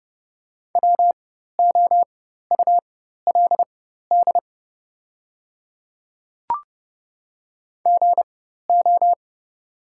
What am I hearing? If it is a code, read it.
Morse code, heard as a keyed tone, about 31 wpm, spaced as for 14 wpm, 700 Hz. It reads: WOULD GO
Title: 4x Character Spacing - Top 100 Words - 30wpm